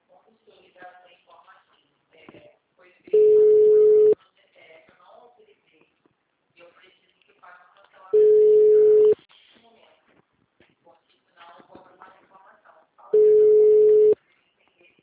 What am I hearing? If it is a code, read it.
Ringback tone (Europe, 425 Hz), with speech.